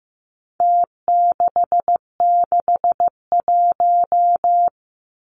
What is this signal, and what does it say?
Morse code, heard as a keyed tone, about 15 wpm, 700 Hz. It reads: T661
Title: Numbers - Contest Style - 4-Digit 15wpm